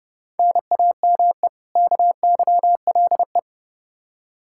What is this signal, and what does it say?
Morse code, heard as a keyed tone, about 30 wpm, 700 Hz. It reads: NAME KYLE
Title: Practice QSO - Exchange 1 - Beginning - 30wpm